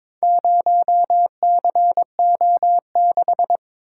Morse code, heard as a keyed tone, about 23 wpm, 690 Hz. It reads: 0CO6
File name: practice / 22wpm / Four Letter-Number Combinations - 22wpm